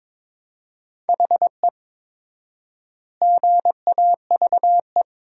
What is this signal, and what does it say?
Morse code, heard as a keyed tone, about 22 wpm, 700 Hz. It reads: HE GAVE
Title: 4x Word-Spacing - Sentences from Top 500 Words - 22wpm